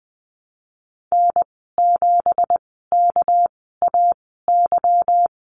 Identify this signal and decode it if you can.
Morse code, heard as a keyed tone, about 20 wpm, spaced as for 15 wpm, 700 Hz. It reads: N7KAY
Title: 207 Lesson 33g - Letter j - Call Signs 20wpm